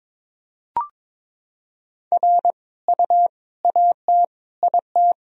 Morse code, heard as a keyed tone, about 22 wpm, 700 Hz. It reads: R U AT IT
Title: Sentences from Top 200 Words and Abbreviations 22wpm